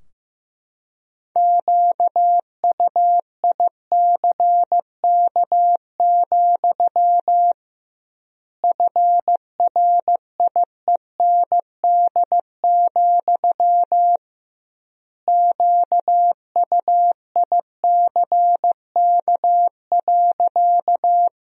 Morse code, heard as a keyed tone, about 15 wpm, 700 Hz. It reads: QUICK, FRIEND, QUICK.
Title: 2x Word-Spacing - Sentences from Top 500 Words - 15wpm